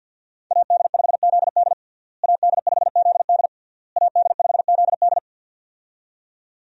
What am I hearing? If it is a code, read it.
Morse code, heard as a keyed tone, about 51 wpm, 700 Hz. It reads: AD5ZD AD5ZD AD5ZD
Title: USA Call Signs - 5 Characters - Repeated 3x - 50wpm